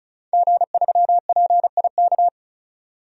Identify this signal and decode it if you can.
Morse code, heard as a keyed tone, about 36 wpm, 700 Hz. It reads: G3PIK
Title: DXCC-100 Call Signs - 5 Characters - 35wpm